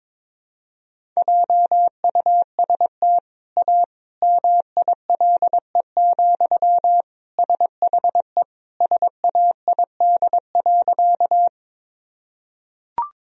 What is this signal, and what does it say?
Morse code, heard as a keyed tone, about 22 wpm, 700 Hz. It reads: JUST A MILE, SHE SAID.
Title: Sentences from Top 1000 Words - Review Part 1 - 22wpm